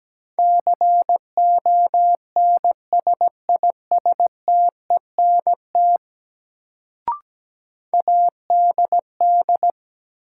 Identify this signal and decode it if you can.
Morse code, heard as a keyed tone, about 17 wpm, 700 Hz. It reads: CONSISTENT ADD
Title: Sentences from Top 1500 Words - New Word Drill Alt - 17wpm